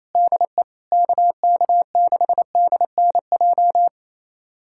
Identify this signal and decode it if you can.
Morse code, heard as a keyed tone, about 28 wpm, 700 Hz. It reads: DE KK6DNJ
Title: USA Callsign DE Callsign - 6 Characters - 28wpm